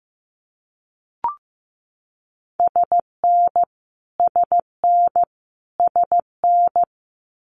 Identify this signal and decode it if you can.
Morse code, heard as a keyed tone, about 15 wpm, 700 Hz. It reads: SN SN SN
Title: Abbreviations Repeated 3x - 15wpm